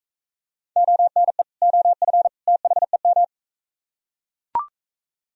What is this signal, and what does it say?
Morse code, heard as a keyed tone, about 42 wpm, 700 Hz. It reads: ONE OF THEM